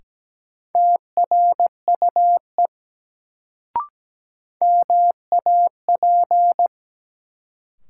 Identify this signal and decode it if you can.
Morse code, heard as a keyed tone, about 17 wpm, 700 Hz. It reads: TRUE MAP
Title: Sentences from Top 400 Words - New Word Drill - 17wpm